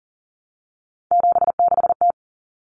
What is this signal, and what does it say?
Morse code, heard as a keyed tone, about 40 wpm, 700 Hz. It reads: 76T
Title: Common Cut Numbers - 3-Digit 40wpm